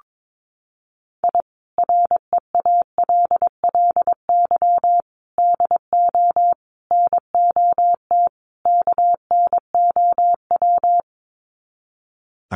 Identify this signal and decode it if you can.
Morse code, heard as a keyed tone, about 22 wpm, 700 Hz. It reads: I REALLY DO NOT KNOW
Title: Sets of 5 Words Encore 22wpm